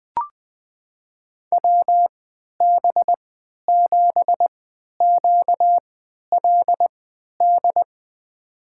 Morse code, heard as a keyed tone, about 21 wpm, spaced as for 11 wpm, 700 Hz. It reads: WB7QLD
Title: USA Call Signs - 6 Characters - 3x Character Spacing - 20wpm